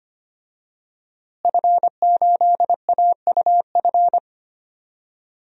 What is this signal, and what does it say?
Morse code, heard as a keyed tone, about 25 wpm, 700 Hz. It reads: F8AUF
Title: DXCC-100 Call Signs - 5 Characters - Repeated 2x - 25wpm